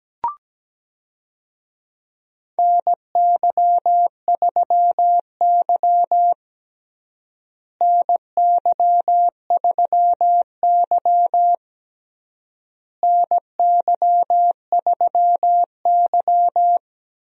Morse code, heard as a keyed tone, about 17 wpm, 700 Hz. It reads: NY3Y NY3Y NY3Y
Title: USA Call Signs - 4 Characters - Repeated 3x - 17wpm